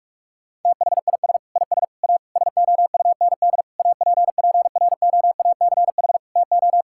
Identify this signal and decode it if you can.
Morse code, heard as a keyed tone, about 46 wpm, 700 Hz. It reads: THIS IS A SOUND APPROACH TO
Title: Sentences from Top 1000 Words - Review Part 6 - 45wpm